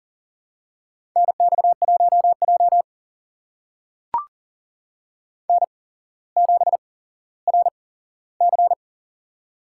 Morse code, heard as a keyed tone, about 40 wpm, spaced as for 17 wpm, 700 Hz. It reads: NX1J N7RC
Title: USA Call Signs - 4 Characters - 8x Character Spacing - 40wpm